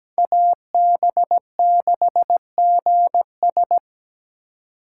Morse code, heard as a keyed tone, about 17 wpm, 700 Hz. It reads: AB6GS
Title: USA Call Signs - 5 Characters - 2x Character Spacing - 17wpm